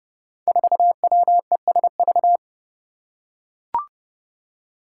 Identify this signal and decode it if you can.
Morse code, heard as a keyed tone, about 30 wpm, 700 Hz. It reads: 4WESV
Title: Five Challenging Letter-Number Combinations 30wpm